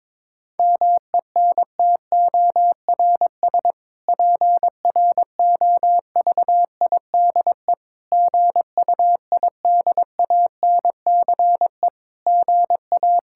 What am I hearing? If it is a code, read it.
Morse code, heard as a keyed tone, about 22 wpm, 700 Hz. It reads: MENTORS PROVIDE GUIDANCE GA
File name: Difficult Sentences with Hints - 22wpm